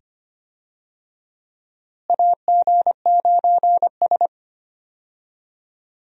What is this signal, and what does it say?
Morse code, heard as a keyed tone, about 25 wpm, 700 Hz. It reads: AG9S
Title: USA Call Signs - 4 Characters - Repeated 2x - 25wpm